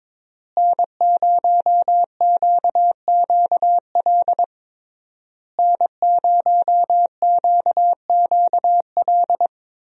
Morse code, heard as a keyed tone, about 22 wpm, 700 Hz. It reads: N0QQL N0QQL
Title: USA Call Signs - 5 Characters - Repeated 3x - 22wpm